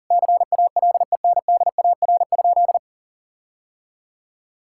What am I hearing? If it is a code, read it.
Morse code, heard as a keyed tone, about 40 wpm, 700 Hz. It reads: CALENDAR?
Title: Sentences from Top 1600 Words 40wpm